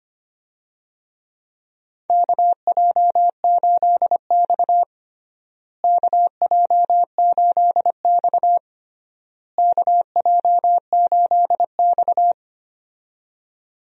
Morse code, heard as a keyed tone, about 25 wpm, 700 Hz. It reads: KJ8X KJ8X KJ8X
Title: USA Call Signs - 4 Characters - Repeated 3x - 25wpm